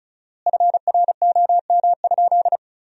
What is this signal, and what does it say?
Morse code, heard as a keyed tone, about 36 wpm, 700 Hz. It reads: FROM?
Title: 4x Word-Spacing - Sentences from Top 300 Words - 35wpm